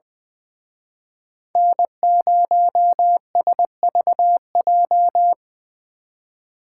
Morse code, heard as a keyed tone, about 24 wpm, 700 Hz. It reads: N0SVJ
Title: Speed Racing - USA 1x3 Call Signs 20wpm